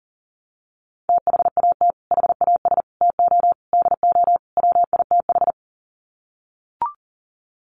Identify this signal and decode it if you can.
Morse code, heard as a keyed tone, about 40 wpm, 700 Hz. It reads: THAT HAS TO DO WITH
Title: Sets of 5 Words Encore2 40wpm